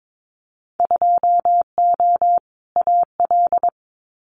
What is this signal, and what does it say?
Morse code, heard as a keyed tone, about 22 wpm, 700 Hz. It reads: 2O AL
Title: Winter Field Day Exchanges Encore 22wpm